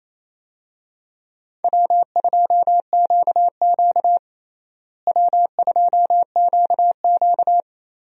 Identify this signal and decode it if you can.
Morse code, heard as a keyed tone, about 28 wpm, 700 Hz. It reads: W2QQ W2QQ
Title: USA Call Signs - 4 Characters - Repeated 3x - 28wpm